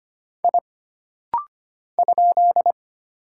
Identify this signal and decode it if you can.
Morse code, heard as a keyed tone, about 25 wpm, 700 Hz. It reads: I ?
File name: Single Character (Letter:Number:Punctuation:Prosign) - Rapid-Fire - 25wpm